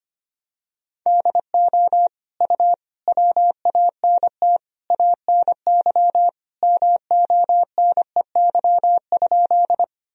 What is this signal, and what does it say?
Morse code, heard as a keyed tone, about 25 wpm, 700 Hz. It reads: DO U WANT ANY MONEY?
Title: Sentences from Top 400 Words and Abbreviations 25wpm